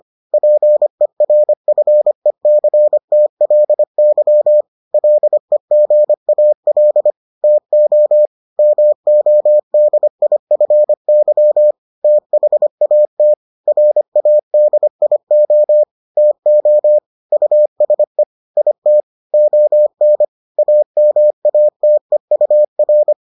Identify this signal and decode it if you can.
Morse code, heard as a keyed tone, about 25 wpm, 600 Hz. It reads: PERFECTLY LEGAL TO MODIFY THAT RADIO TO USE IT ON AMATEUR